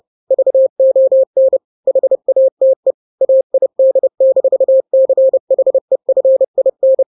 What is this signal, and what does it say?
Morse code, heard as a keyed tone, about 30 wpm, 530 Hz. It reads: VON HATE AID-CHEFIN
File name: daily / 29wpm